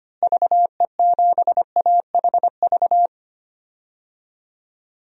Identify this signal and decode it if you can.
Morse code, heard as a keyed tone, about 25 wpm, 700 Hz. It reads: VE7AHV
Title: DXCC-100 Call Signs - 6 Characters - Repeated 2x - 25wpm